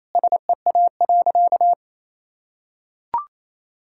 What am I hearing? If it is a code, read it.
Morse code, heard as a keyed tone, about 28 wpm, 700 Hz. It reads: SEA.
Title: Sentences from Top 1000 Words - Review Part 6 - 28wpm